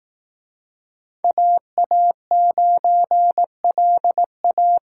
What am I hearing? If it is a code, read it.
Morse code, heard as a keyed tone, about 18 wpm, 700 Hz. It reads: AA9LA
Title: Speed Racing - USA 2x2 Call Signs 15wpm